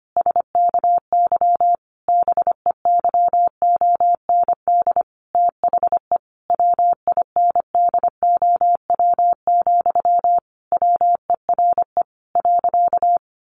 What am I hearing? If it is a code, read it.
Morse code, heard as a keyed tone, about 25 wpm, 700 Hz. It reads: SKY BEYOND THE WINDOW, WERE .
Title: A Sound Of Thunder 25wpm